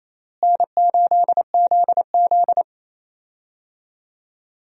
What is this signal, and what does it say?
Morse code, heard as a keyed tone, about 27 wpm, 700 Hz. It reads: N8ZZ
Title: USA Call Signs - 4 Characters - Repeated 2x - 28wpm